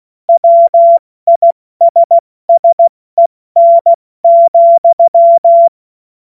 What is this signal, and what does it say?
Morse code, heard as a keyed tone, about 16 wpm, spaced as for 14 wpm, 680 Hz. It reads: WISSEN,
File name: Witz-26-August-2025-70-F80